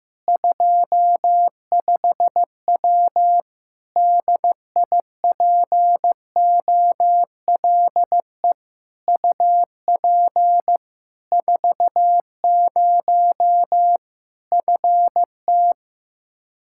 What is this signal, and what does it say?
Morse code, heard as a keyed tone, about 15 wpm, 700 Hz. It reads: 25W DIPOLE UP 40 FT
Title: K1USN SST QSO Practice 15wpm